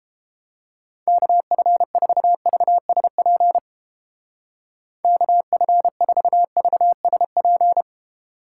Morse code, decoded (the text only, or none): KF4VSP KF4VSP